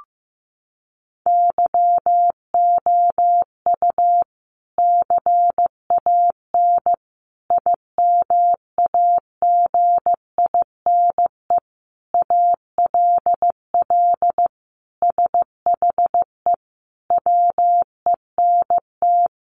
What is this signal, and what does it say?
Morse code, heard as a keyed tone, about 15 wpm, 700 Hz. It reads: YOU CAN IMAGINE ALL SHE WENT
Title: Sentences from Top 1400 Words 15wpm